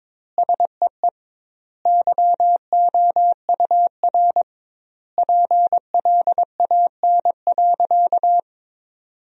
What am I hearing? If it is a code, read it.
Morse code, heard as a keyed tone, about 21 wpm, 700 Hz. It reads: SEE YOUR PLAN.
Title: 2x Word-Spacing - Sentences from Top 500 Words - 22wpm